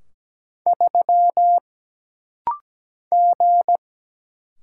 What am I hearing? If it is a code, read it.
Morse code, heard as a keyed tone, about 17 wpm, 700 Hz. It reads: 3 G
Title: Single Letter-Number Alt - Rapid-Fire 17wpm